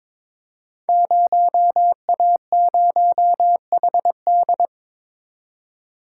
Morse code, heard as a keyed tone, about 22 wpm, 700 Hz. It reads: 0A0HD
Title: Code Groups - 5 Letter:Numbers - 22wpm